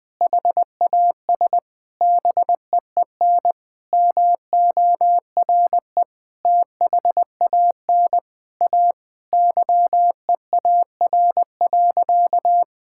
Morse code, heard as a keyed tone, about 20 wpm, 700 Hz. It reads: HAS BEEN MORE THAN A YEAR.
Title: Sentences with 4-Word Phrases - 20wpm